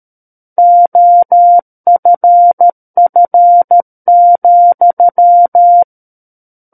Morse code, heard as a keyed tone, about 13 wpm, 700 Hz. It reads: OFF,
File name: Book One / 04-Chapter 4 - 13 WPM